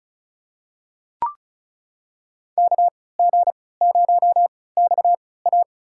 Morse code, heard as a keyed tone, about 34 wpm, spaced as for 20 wpm, 700 Hz. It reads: KG0XA
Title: USA Call Signs - 5 Characters - 3x Character Spacing - 35wpm